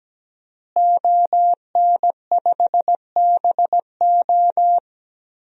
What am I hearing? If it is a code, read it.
Morse code, heard as a keyed tone, about 17 wpm, 700 Hz. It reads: ON5BO
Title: DXCC-100 Call Signs - 5 Characters - Repeated 2x - 17wpm